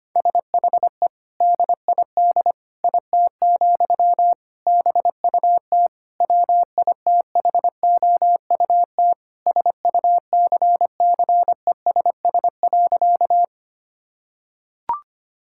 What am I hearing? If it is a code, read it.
Morse code, heard as a keyed tone, about 25 wpm, 700 Hz. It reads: SHE DID IT, BUT WITHOUT SUCCESS.
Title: Sentences from Top 1000 Words - Review Part 6 - 25wpm